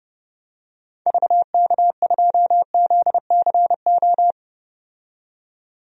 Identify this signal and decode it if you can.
Morse code, heard as a keyed tone, about 30 wpm, 700 Hz. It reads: VK2ZCO
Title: DXCC-100 Call Signs - 6 Characters - 30wpm